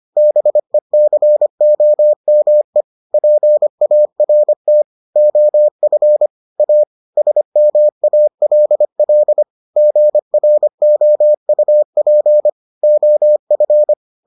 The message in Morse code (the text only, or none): BECOME PART OF A SMALL GROUP OF